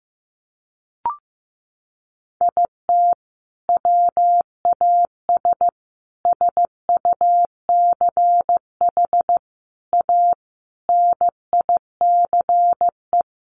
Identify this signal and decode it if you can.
Morse code, heard as a keyed tone, about 15 wpm, 700 Hz. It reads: IT WAS SUCH A NICE
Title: Sentences from Top 1400 Words 15wpm